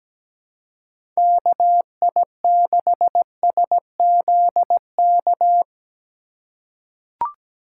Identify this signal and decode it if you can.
Morse code, heard as a keyed tone, about 17 wpm, 700 Hz. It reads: KI6SZK
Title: USA Call Signs - 6 Characters - 4x Character Spacing - 17wpm